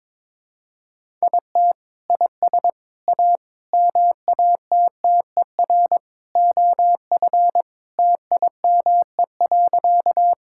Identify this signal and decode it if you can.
Morse code, heard as a keyed tone, about 22 wpm, 700 Hz. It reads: IT IS A MATTER OF TIME.
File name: practice / 22wpm / Sentences with 3-Word Phrases - 22wpm